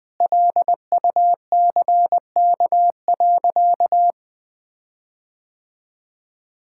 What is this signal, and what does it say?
Morse code, heard as a keyed tone, about 20 wpm, 700 Hz. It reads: LUCK.
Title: Sentences with 8-Word Phrases - 20wpm